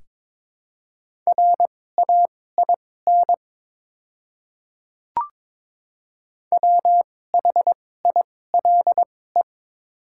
Morse code, heard as a keyed tone, about 21 wpm, spaced as for 16 wpm, 700 Hz. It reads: RAIN WHILE
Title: 2x Character Spacing - Top 300 Words - 22wpm